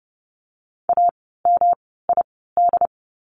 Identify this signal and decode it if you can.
Morse code, heard as a keyed tone, about 30 wpm, spaced as for 17 wpm, 700 Hz. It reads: AMID
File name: course / 30:10wpm / 108 Lesson 18f - Letter m - Words 30wpm